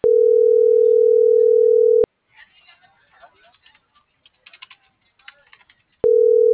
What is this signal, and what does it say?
Ringback tone (North America), 440 + 480 Hz.